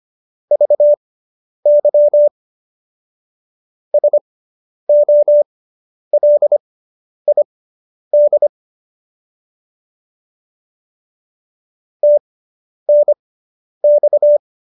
Morse code, heard as a keyed tone, about 25 wpm, spaced as for 10 wpm, 590 Hz. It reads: VY SOLID TNX